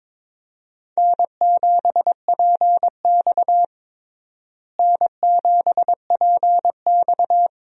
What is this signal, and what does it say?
Morse code, heard as a keyed tone, about 22 wpm, 700 Hz. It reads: N7PX N7PX
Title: USA Call Signs - 4 Characters - Repeated 2x - 22wpm